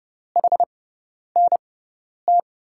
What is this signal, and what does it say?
Morse code, heard as a keyed tone, about 30 wpm, spaced as for 10 wpm, 700 Hz. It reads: HNT
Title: USA Call Signs - 6 Characters - 6x Character Spacing - 30wpm